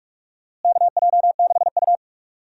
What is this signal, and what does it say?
Morse code, heard as a keyed tone, about 44 wpm, 700 Hz. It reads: KJ6U
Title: USA Call Signs - 4 Characters - Repeated 2x - 45wpm